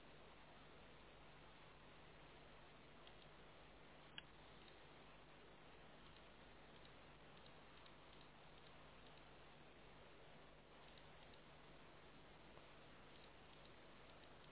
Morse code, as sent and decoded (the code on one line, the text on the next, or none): none